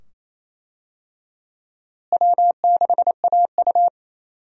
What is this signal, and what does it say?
Morse code, heard as a keyed tone, about 28 wpm, 700 Hz. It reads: W6AU
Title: USA Call Signs - 4 Characters - Repeated 2x - 28wpm